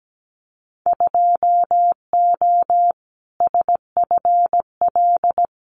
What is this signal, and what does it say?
Morse code, heard as a keyed tone, about 17 wpm, 700 Hz. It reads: 2O SFL
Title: Winter Field Day Exchanges 17wpm